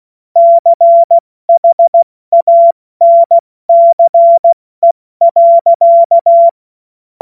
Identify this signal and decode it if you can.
Morse code, heard as a keyed tone, about 16 wpm, 680 Hz. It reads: CHANCE.
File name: Morsekurs / Witze / Witz 28.November 2023 70 F80